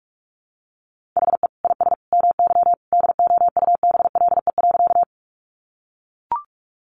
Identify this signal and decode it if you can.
Morse code, heard as a keyed tone, about 44 wpm, 700 Hz. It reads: HE IS MY DOUBLE.
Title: Sentences from Top 1400 Words 45wpm